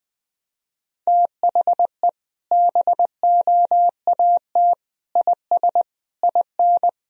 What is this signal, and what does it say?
Morse code, heard as a keyed tone, about 21 wpm, 700 Hz. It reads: THE BOAT IS IN
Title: Sentences from Top 1000 Words - Review Part 2 - 20wpm